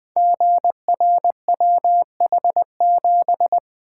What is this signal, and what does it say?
Morse code, heard as a keyed tone, about 20 wpm, 700 Hz. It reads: GRWH7
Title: Code Groups - 5 Letter:Numbers - 20wpm